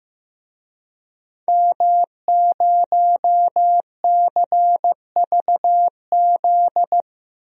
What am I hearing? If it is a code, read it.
Morse code, heard as a keyed tone, about 15 wpm, 700 Hz. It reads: M0CVZ
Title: DXCC-100 Call Signs - 5 Characters - 15wpm